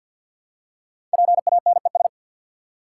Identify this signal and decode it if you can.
Morse code, heard as a keyed tone, about 50 wpm, 700 Hz. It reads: PANES
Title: Code Groups - 5 Letters - 50wpm